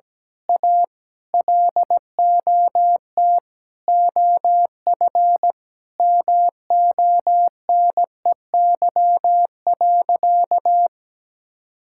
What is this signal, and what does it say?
Morse code, heard as a keyed tone, about 17 wpm, 700 Hz. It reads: A LOT OF MONEY.
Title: Sentences from Top 1000 Words - Review Part 6 - 17wpm